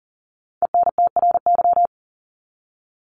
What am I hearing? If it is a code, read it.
Morse code, heard as a keyed tone, about 40 wpm, 700 Hz. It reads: ENTRY
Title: Top 1251-1500 Words 40wpm